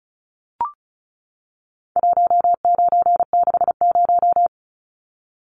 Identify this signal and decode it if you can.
Morse code, heard as a keyed tone, about 34 wpm, 700 Hz. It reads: 1960
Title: Numbers - 4-Digit 35wpm